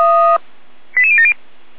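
Five-tone selective call (ZVEI), heard as quick runs of five tones, tones of about 70 ms. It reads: 88070
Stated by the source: Five-tone ZVEI selective call received on SDR.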